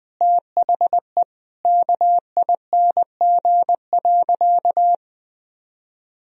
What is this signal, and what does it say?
Morse code, heard as a keyed tone, about 20 wpm, 700 Hz. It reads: THE KING.
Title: Sentences from Top 1000 Words - Review Part 2 - 20wpm